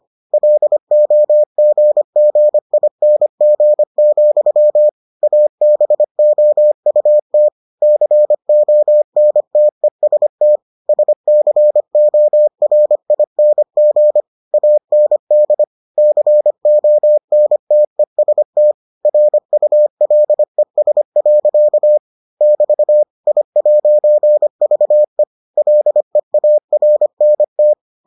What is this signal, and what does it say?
Morse code, heard as a keyed tone, about 25 wpm, 600 Hz. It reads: LOGGING, ABOUT CONTEST SCORING AND CONTEST RULES. <BT> I'VE LEARNT